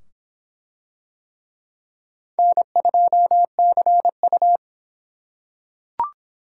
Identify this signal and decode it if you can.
Morse code, heard as a keyed tone, about 24 wpm, 700 Hz. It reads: N2CU